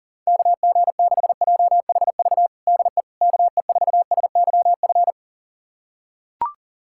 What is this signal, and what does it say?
Morse code, heard as a keyed tone, about 37 wpm, 700 Hz. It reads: KG6JHV DE KE4SYF
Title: USA Callsign DE Callsign - 6 Characters - 40wpm